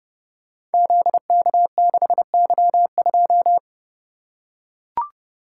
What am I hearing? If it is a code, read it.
Morse code, heard as a keyed tone, about 30 wpm, 700 Hz. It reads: ZK6Y2
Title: Code Groups - 5 Letter:Numbers - 30wpm